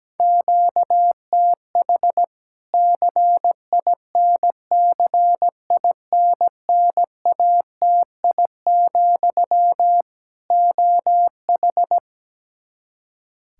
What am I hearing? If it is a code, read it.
Morse code, heard as a keyed tone, about 17 wpm, 700 Hz. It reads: QTH CINCINNATI, OH
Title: Practice QSO - Exchange 1 - Beginning - 17wpm